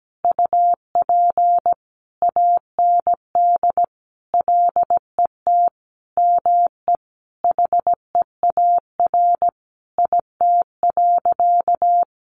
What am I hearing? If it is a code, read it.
Morse code, heard as a keyed tone, about 17 wpm, 700 Hz. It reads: UP AND LET ME HEAR IT.
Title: Sentences from Top 1500 Words 17wpm